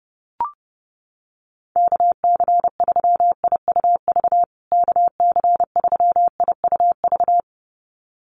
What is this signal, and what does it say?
Morse code, heard as a keyed tone, about 30 wpm, 700 Hz. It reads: KC3IUV KC3IUV
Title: USA Call Signs Repeated 2x 30wpm